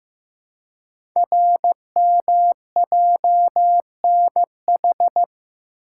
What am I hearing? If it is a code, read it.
Morse code, heard as a keyed tone, about 15 wpm, 700 Hz. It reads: RMJNH
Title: Code Groups - 5 Letters - 15wpm